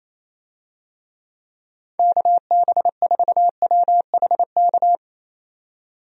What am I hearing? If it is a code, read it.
Morse code, heard as a keyed tone, about 27 wpm, 700 Hz. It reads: KB4WHK
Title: USA Call Signs - 6 Characters - 28wpm